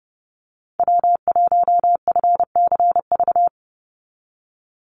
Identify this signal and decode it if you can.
Morse code, heard as a keyed tone, about 30 wpm, 700 Hz. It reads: W1FCV
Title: Call Signs - 30wpm